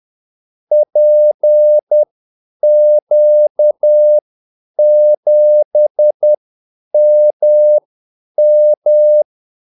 Morse code, heard as a keyed tone, about 10 wpm, spaced as for 8 wpm, 600 Hz. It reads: PQ7MM
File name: Callsigns-10-8wpm